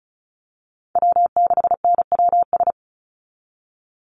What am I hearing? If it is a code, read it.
Morse code, heard as a keyed tone, about 34 wpm, 700 Hz. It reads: W6NWS
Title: Call Signs Repeated 2x - 35wpm